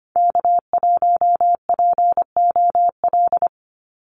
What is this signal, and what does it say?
Morse code, heard as a keyed tone, about 25 wpm, 700 Hz. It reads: K1POL
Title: USA Call Signs 25wpm